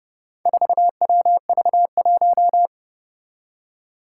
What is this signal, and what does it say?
Morse code, heard as a keyed tone, about 30 wpm, 700 Hz. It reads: 4WV1
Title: Four Challenging Letter-Number Combinations 30wpm